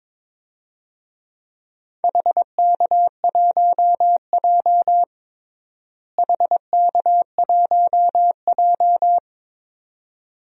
Morse code, heard as a keyed tone, about 22 wpm, 700 Hz. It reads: HK1J HK1J
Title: DXCC-100 Call Signs - 4 Characters - Repeated 2x - 22wpm